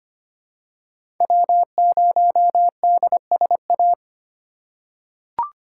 Morse code, heard as a keyed tone, about 25 wpm, 700 Hz. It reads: W0DSA
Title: USA Call Signs - 3x Character Spacing - 25wpm